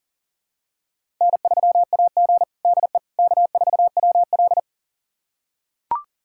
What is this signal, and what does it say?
Morse code, heard as a keyed tone, about 40 wpm, 700 Hz. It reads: N3AG DE K4WL
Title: USA Callsign DE Callsign - 4 Characters - 40wpm